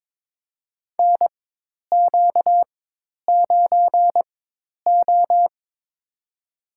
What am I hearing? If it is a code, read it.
Morse code, heard as a keyed tone, about 23 wpm, spaced as for 10 wpm, 700 Hz. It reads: NQ9O
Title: USA Call Signs - 4x Character Spacing - 22wpm